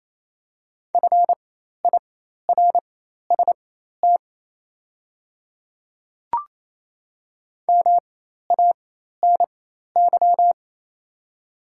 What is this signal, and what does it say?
Morse code, heard as a keyed tone, about 28 wpm, spaced as for 13 wpm, 700 Hz. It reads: FIRST MANY
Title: 4x Character Spacing - Top 200 Words - 28wpm